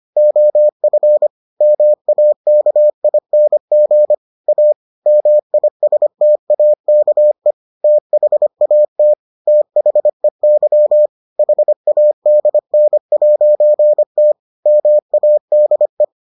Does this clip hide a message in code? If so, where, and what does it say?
Yes: Morse code from 0.2 to 16.0 s, heard as a keyed tone, about 25 wpm, 600 Hz. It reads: OF MAKING A MISTAKE THAT THEY HADN'T MADE